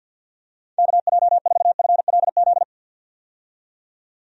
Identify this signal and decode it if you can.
Morse code, heard as a keyed tone, about 49 wpm, 700 Hz. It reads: KJ4FLZ
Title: USA Call Signs - 6 Characters - 50wpm